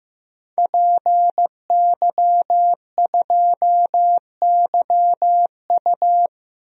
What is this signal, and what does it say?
Morse code, heard as a keyed tone, about 15 wpm, 700 Hz. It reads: PY2YU
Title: DXCC-100 Call Signs - 5 Characters - Repeated 3x - 15wpm